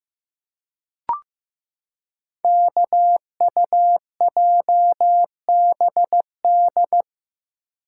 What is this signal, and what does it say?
Morse code, heard as a keyed tone, about 15 wpm, 700 Hz. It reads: KUJBD
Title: Code Groups - 5 Letter:Numbers - 15wpm